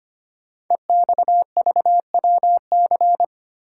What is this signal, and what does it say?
Morse code, heard as a keyed tone, about 25 wpm, 700 Hz. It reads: EXVWC
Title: Code Groups - 5 Letters - 25wpm